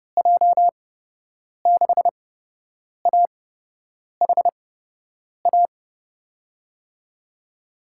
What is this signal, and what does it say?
Morse code, heard as a keyed tone, about 31 wpm, spaced as for 8 wpm, 700 Hz. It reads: J6AHA
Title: USA Call Signs - 6 Characters - 8x Character Spacing - 30wpm